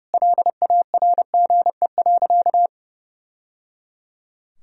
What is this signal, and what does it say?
Morse code, heard as a keyed tone, about 29 wpm, 700 Hz. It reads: LARGE.
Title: Sentences from Top 1000 Words - Review Part 1 - 30wpm